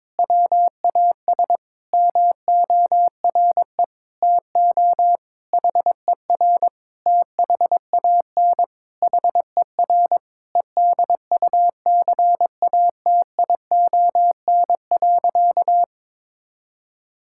Morse code, heard as a keyed tone, about 22 wpm, 700 Hz. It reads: WAS MORE TO HER THAN HER EDUCATION.
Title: Sentences from Top 1000 Words - Review Part 6 - 22wpm